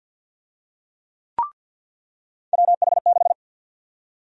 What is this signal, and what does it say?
Morse code, heard as a keyed tone, about 52 wpm, 700 Hz. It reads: WH6
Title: USA Call Sign Prefixes - 50wpm